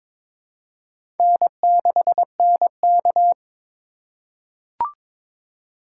Morse code, heard as a keyed tone, about 22 wpm, 700 Hz. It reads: N6NK
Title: USA Call Signs - 3x Character Spacing - 22wpm